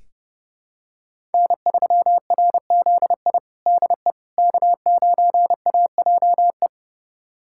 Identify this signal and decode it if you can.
Morse code, heard as a keyed tone, about 32 wpm, 700 Hz. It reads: N3RZI DE K9AJE
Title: USA Callsign DE Callsign - 5 Characters - 30wpm